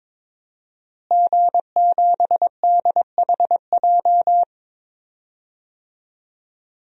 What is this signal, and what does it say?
Morse code, heard as a keyed tone, about 22 wpm, 700 Hz. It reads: G7DHJ
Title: DXCC-100 Call Signs - 5 Characters - Repeated 3x - 22wpm